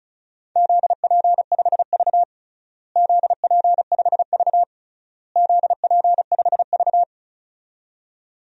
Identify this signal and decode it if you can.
Morse code, heard as a keyed tone, about 36 wpm, 700 Hz. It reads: ZP5V ZP5V ZP5V
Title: DXCC-100 Call Signs - 4 Characters - Repeated 3x - 35wpm